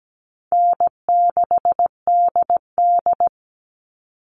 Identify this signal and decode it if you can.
Morse code, heard as a keyed tone, about 17 wpm, 700 Hz. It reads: N6DD
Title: All Cut Numbers - 4-Digit 17wpm